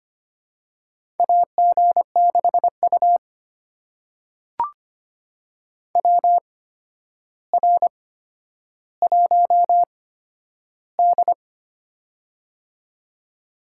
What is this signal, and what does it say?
Morse code, heard as a keyed tone, about 25 wpm, spaced as for 11 wpm, 700 Hz. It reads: AG6U WR1D
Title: USA Call Signs - 4 Characters - 8x Character Spacing - 25wpm